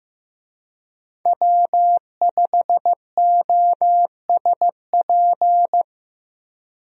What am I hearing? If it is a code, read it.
Morse code, heard as a keyed tone, about 15 wpm, 700 Hz. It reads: W5OSP
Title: USA Call Signs - 5 Characters - Repeated 3x - 15wpm